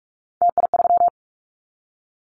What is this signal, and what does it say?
Morse code, heard as a keyed tone, about 44 wpm, 700 Hz. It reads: TI3
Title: Callsign Prefixes for Top 100 DXCC - 45wpm